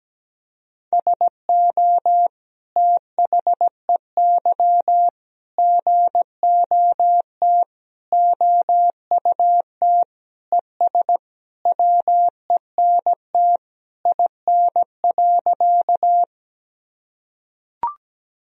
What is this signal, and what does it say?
Morse code, heard as a keyed tone, about 17 wpm, 700 Hz. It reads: SO THEY GOT OUT ES WENT IN.
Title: Sentences from Top 300 Words and Abbreviations 17wpm